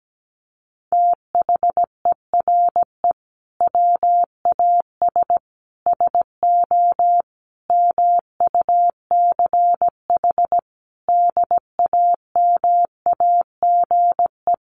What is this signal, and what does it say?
Morse code, heard as a keyed tone, about 17 wpm, 700 Hz. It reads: THERE WAS SO MUCH DAMAGE
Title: Sentences from Top 1100 Words 17wpm